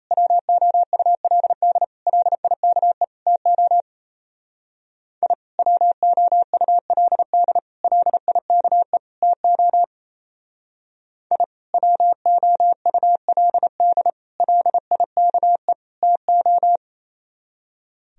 Morse code, heard as a keyed tone, about 33 wpm, 700 Hz. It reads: WOULD LIKE TO I WOULD LIKE TO I WOULD LIKE TO
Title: Speed-Racing - Sets of 4 Words 28wpm